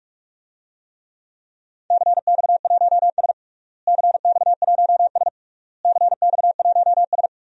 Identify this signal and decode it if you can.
Morse code, heard as a keyed tone, about 42 wpm, 700 Hz. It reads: CX1S CX1S CX1S
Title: DXCC-100 Call Signs - 4 Characters - Repeated 3x - 45wpm